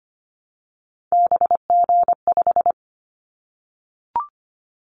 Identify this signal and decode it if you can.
Morse code, heard as a keyed tone, about 25 wpm, 700 Hz. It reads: BG5
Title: Callsign Prefixes for Top 100 DXCC - 25wpm